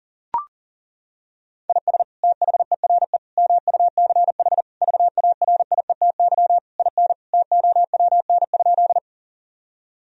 Morse code, heard as a keyed tone, about 37 wpm, 700 Hz. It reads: IS THERE MUCH VARIETY IN TOWN?
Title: Sentences from Top 1000 Words - Review Part 4 - 40wpm